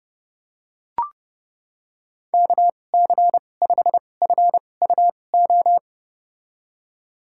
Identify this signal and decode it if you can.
Morse code, heard as a keyed tone, about 32 wpm, spaced as for 22 wpm, 700 Hz. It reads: KC5FUO
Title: USA Call Signs - 6 Characters - 2x Character Spacing - 30wpm